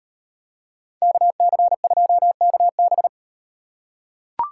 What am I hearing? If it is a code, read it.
Morse code, heard as a keyed tone, about 37 wpm, 700 Hz. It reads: KC2KB